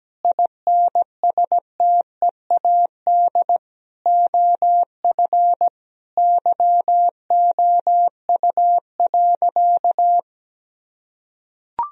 Morse code, heard as a keyed tone, about 17 wpm, 700 Hz. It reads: INSTEAD OF YOU.
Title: Sentences from Top 1000 Words - Review Part 4 - 17wpm